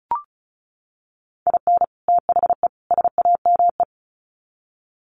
Morse code, heard as a keyed tone, about 34 wpm, 700 Hz. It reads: IN THE SAME
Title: Sets of 3 Words 35wpm